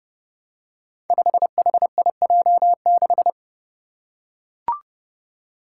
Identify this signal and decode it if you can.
Morse code, heard as a keyed tone, about 30 wpm, 700 Hz. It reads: 5HIJ6
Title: Five Challenging Letter-Number Combinations 30wpm